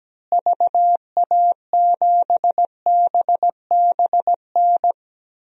Morse code, heard as a keyed tone, about 17 wpm, 700 Hz. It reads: VA7BBN
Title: DXCC-100 Call Signs - 6 Characters - Repeated 3x - 17wpm